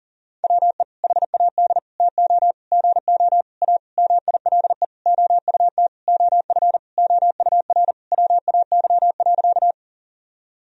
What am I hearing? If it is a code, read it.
Morse code, heard as a keyed tone, about 40 wpm, 700 Hz. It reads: WE HAD TO GO A MILE OUT OF OUR WAY.
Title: Sentences from Top 1000 Words - Review Part 1 - 40wpm